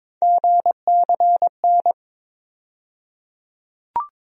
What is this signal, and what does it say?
Morse code, heard as a keyed tone, about 22 wpm, 700 Hz. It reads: GCN
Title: USA Call Sign Postfixes - 3 Letters 22wpm